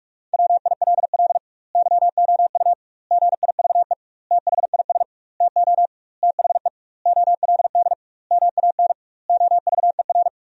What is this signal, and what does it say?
Morse code, heard as a keyed tone, about 46 wpm, 700 Hz. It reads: WILL YOU GIVE THIS TO THE OLD MAN OVER
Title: 2x Word-Spacing - Sentences from Top 200 Words - 45wpm